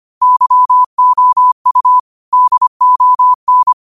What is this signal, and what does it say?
Morse code, heard as a keyed tone, about 25 wpm, 1000 Hz. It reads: YOU DON
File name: qotdfiles / 25wpm